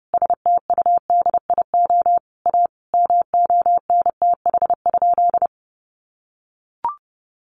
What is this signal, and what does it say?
Morse code, heard as a keyed tone, about 30 wpm, 700 Hz. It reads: STUDIO A MONTH?
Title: Sentences from Top 1200 Words 30wpm